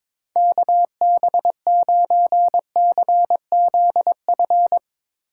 Morse code, heard as a keyed tone, about 22 wpm, 700 Hz. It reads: KB9CZF
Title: Speed Racing - USA 2x3 Call Signs 22wpm